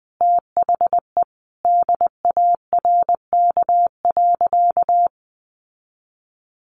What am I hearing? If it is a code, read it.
Morse code, heard as a keyed tone, about 20 wpm, 700 Hz. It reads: THE DARK.
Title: Sentences from Top 1100 Words 20wpm